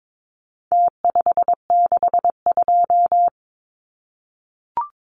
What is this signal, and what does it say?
Morse code, heard as a keyed tone, about 22 wpm, 700 Hz. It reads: T562
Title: Numbers - Contest Style - 4-Digit 22wpm